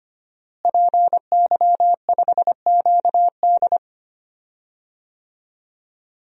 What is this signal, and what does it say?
Morse code, heard as a keyed tone, about 25 wpm, 700 Hz. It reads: PY5QD
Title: DXCC-100 Call Signs - 5 Characters - Repeated 3x - 25wpm